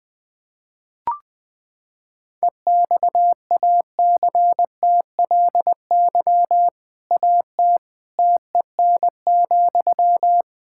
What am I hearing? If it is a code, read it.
Morse code, heard as a keyed tone, about 21 wpm, 700 Hz. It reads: EXACTLY AT TEN,
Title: Sentences from Top 1000 Words - Review Part 6 - 20wpm